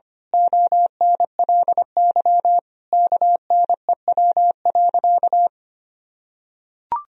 Morse code, heard as a keyed tone, about 25 wpm, 700 Hz. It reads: ONLY KNEW.
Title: Sentences from Top 400 Words and Abbreviations 25wpm